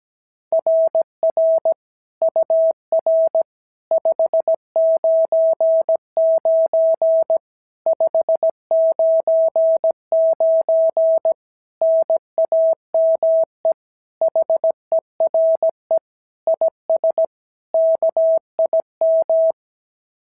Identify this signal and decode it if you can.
Morse code, heard as a keyed tone, about 17 wpm, 650 Hz. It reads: RR UR 599 599 NAME HERE IS KIM
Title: SOTA QSO Practice 17wpm